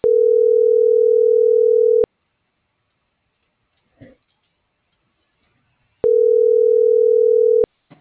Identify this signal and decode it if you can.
Ringback tone (North America), 440 + 480 Hz.